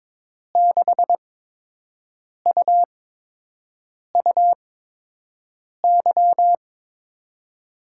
Morse code, heard as a keyed tone, about 22 wpm, spaced as for 6 wpm, 700 Hz. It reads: 6UUY
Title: USA Call Signs - 6 Characters - 8x Character Spacing - 22wpm